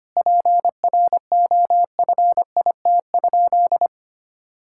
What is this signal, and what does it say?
Morse code, heard as a keyed tone, about 25 wpm, 700 Hz. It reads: PROFIT?
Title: Sentences from Top 1000 Words - Review Part 5 - 25wpm